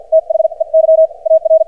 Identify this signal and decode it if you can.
Morse code, heard as a keyed tone, about 50 wpm, spaced as for 41 wpm, 590 Hz. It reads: THEY AR
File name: archived / CRSnet